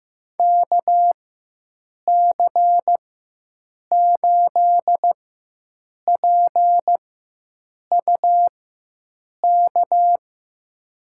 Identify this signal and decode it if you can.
Morse code, heard as a keyed tone, about 15 wpm, spaced as for 7 wpm, 700 Hz. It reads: KC8PUK
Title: USA Call Signs - 6 Characters - 4x Character Spacing - 15wpm